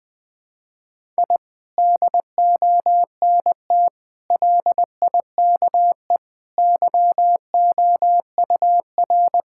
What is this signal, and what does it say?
Morse code, heard as a keyed tone, about 20 wpm, 700 Hz. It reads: I DONT LIKE YOUR
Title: Sentences from Top 1000 Words - Review Part 6 - 20wpm